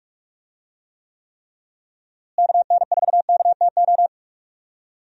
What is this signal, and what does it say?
Morse code, heard as a keyed tone, about 42 wpm, 700 Hz. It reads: KN4KTO
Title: USA Call Signs - 6 Characters - Repeated 3x - 45wpm